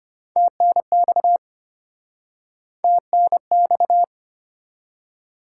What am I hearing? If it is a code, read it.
Morse code, heard as a keyed tone, about 32 wpm, 700 Hz. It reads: TNX TNX